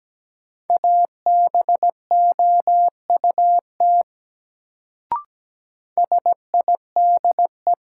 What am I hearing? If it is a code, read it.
Morse code, heard as a keyed tone, about 17 wpm, 700 Hz. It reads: ABOUT SIDE
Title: Sets of 2 Words - New Word Drill - 17wpm